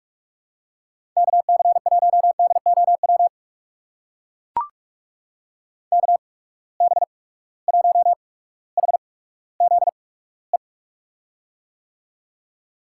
Morse code, heard as a keyed tone, about 44 wpm, spaced as for 19 wpm, 700 Hz. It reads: KC1DOW KB1HZE
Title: USA Call Signs - 6 Characters - 8x Character Spacing - 45wpm